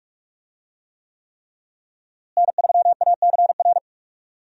Morse code, heard as a keyed tone, about 43 wpm, 700 Hz. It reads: N3ACR
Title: USA Call Signs - 5 Characters - Repeated 3x - 45wpm